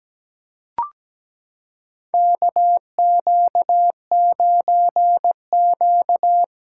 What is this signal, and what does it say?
Morse code, heard as a keyed tone, about 17 wpm, 700 Hz. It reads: KQ9Q
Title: USA Callsign DE Callsign - 4 Characters - 17wpm